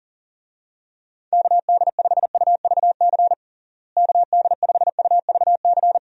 Morse code, heard as a keyed tone, about 37 wpm, 700 Hz. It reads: KD5UVC KD5UVC
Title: USA Call Signs - 6 Characters - Repeated 3x - 40wpm